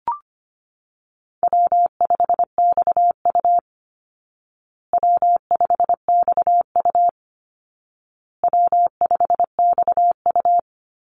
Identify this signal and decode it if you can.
Morse code, heard as a keyed tone, about 25 wpm, 700 Hz. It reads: W5XU W5XU W5XU
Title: Easy Call Signs Repeated 3x - 25wpm